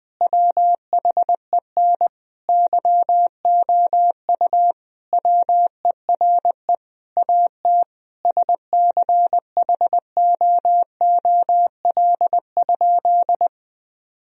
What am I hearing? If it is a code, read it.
Morse code, heard as a keyed tone, about 21 wpm, 700 Hz. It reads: WHEN YOU WERE AT SCHOOL?
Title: Sentences from Top 1000 Words - Review Part 1 - 20wpm